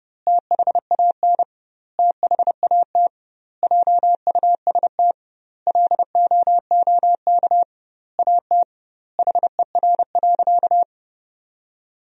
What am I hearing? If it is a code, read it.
Morse code, heard as a keyed tone, about 30 wpm, 700 Hz. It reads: THAN THAT JUST LOOK AT HER.
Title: 2x Word-Spacing - Sentences from Top 200 Words - 30wpm